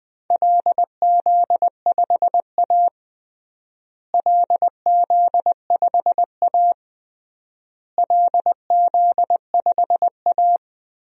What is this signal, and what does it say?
Morse code, heard as a keyed tone, about 20 wpm, 700 Hz. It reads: LZ5A LZ5A LZ5A
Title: DXCC-100 Call Signs - 4 Characters - Repeated 3x - 20wpm